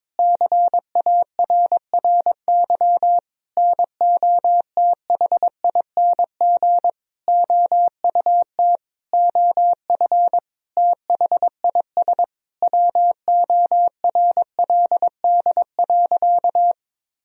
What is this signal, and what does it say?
Morse code, heard as a keyed tone, about 22 wpm, 700 Hz. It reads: CARRY NOTHING OUT OF THIS WORLD.
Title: Sentences from Top 1000 Words - Review Part 1 - 22wpm